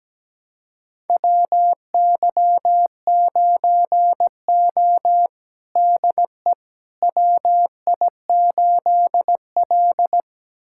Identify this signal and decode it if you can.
Morse code, heard as a keyed tone, about 17 wpm, 700 Hz. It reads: WY9O DE WI8L
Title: USA Callsign DE Callsign - 4 Characters - 17wpm